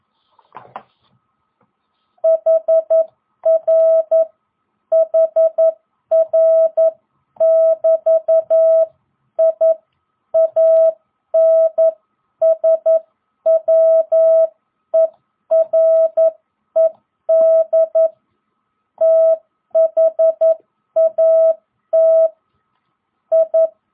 Morse code, heard as a keyed tone, about 11 wpm, 660 Hz. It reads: HR HR <BT> I ANSWERED THAT I